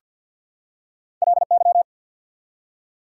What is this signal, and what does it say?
Morse code, heard as a keyed tone, about 48 wpm, 690 Hz. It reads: LY
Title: Prefixes and Suffixes 50wpm